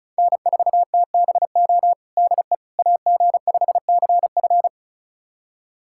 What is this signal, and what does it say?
Morse code, heard as a keyed tone, about 36 wpm, 700 Hz. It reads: N4TBO DE AG5CF
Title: USA Callsign DE Callsign - 5 Characters - 35wpm